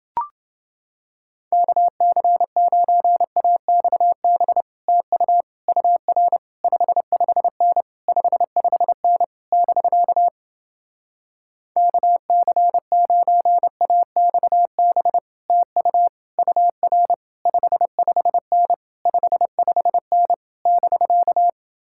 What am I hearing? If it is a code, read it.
Morse code, heard as a keyed tone, about 29 wpm, 700 Hz. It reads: KC9AXB TU UR 55N 55N <BK> KC9AXB TU UR 55N 55N <BK>